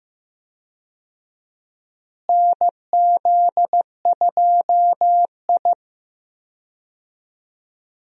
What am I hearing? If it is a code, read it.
Morse code, heard as a keyed tone, about 15 wpm, 700 Hz. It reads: NZ2I
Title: USA Call Signs - 4 Characters - Repeated 2x - 15wpm